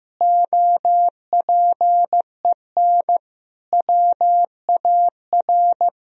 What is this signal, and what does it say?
Morse code, heard as a keyed tone, about 16 wpm, 700 Hz. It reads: OPEN WAR
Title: Sentences from Top 1000 Words - Review Part 2 - 15wpm